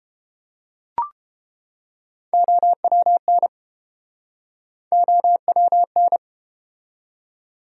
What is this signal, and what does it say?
Morse code, heard as a keyed tone, about 30 wpm, 700 Hz. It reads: OWN OWN